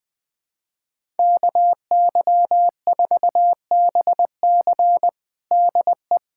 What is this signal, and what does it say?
Morse code, heard as a keyed tone, about 20 wpm, 700 Hz. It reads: KY4BC DE
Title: USA Callsign DE Callsign - 5 Characters - 20wpm